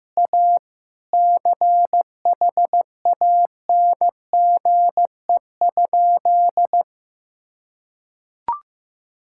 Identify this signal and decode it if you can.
Morse code, heard as a keyed tone, about 15 wpm, 700 Hz. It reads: A CHANGE?
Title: Sentences from Top 1600 Words 15wpm